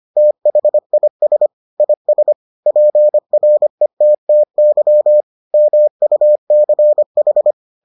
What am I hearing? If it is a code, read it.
Morse code, heard as a keyed tone, about 25 wpm, 600 Hz. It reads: THIS IS PRETTY MUCH